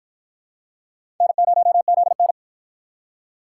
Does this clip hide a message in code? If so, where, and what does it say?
Yes: Morse code from 1.2 to 2.3 s, heard as a keyed tone, about 50 wpm, 700 Hz. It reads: N0ZN